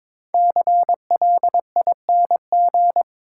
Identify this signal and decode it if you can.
Morse code, heard as a keyed tone, about 22 wpm, 690 Hz. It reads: CLING
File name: 5 Letter Words - First 3 Letters Given - 22wpm